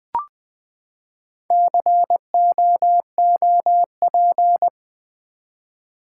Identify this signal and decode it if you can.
Morse code, heard as a keyed tone, about 20 wpm, 700 Hz. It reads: COOP
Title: Build A Word - 4 Letters At A Time - 11-Letter Words - 20wpm